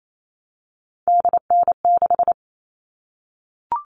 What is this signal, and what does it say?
Morse code, heard as a keyed tone, about 28 wpm, 700 Hz. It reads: DN6
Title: All Cut Numbers - 3-Digit 28wpm